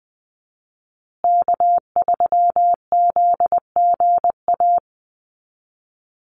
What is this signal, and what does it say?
Morse code, heard as a keyed tone, about 20 wpm, 700 Hz. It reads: K3ZGA
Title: Call Signs Repeated 3x - 20wpm